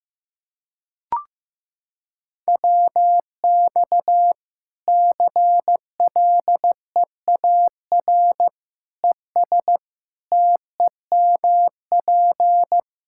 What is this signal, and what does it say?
Morse code, heard as a keyed tone, about 16 wpm, 700 Hz. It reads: WX CLEAR ES TEMP
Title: Practice QSO - Exchange 2 - Info - 15wpm